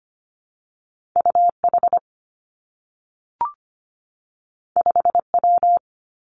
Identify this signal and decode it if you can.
Morse code, heard as a keyed tone, about 25 wpm, 700 Hz. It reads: U H 5 W
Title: Two Challenging Letter-Number Combinations 25wpm